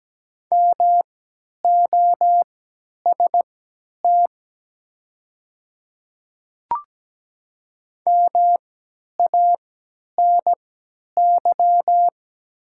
Morse code, heard as a keyed tone, about 17 wpm, spaced as for 10 wpm, 700 Hz. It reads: MOST MANY
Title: 3x Character Spacing - Top 100 Words - 17wpm